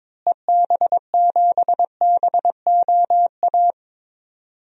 Morse code, heard as a keyed tone, about 21 wpm, 700 Hz. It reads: EB7BOA
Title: DXCC-100 Call Signs - 6 Characters - Repeated 3x - 22wpm